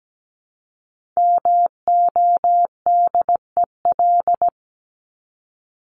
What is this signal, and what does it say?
Morse code, heard as a keyed tone, about 17 wpm, 700 Hz. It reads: MODEL
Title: Top 501-750 Words 17wpm